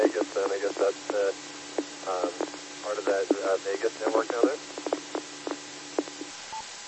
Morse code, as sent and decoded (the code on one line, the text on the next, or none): none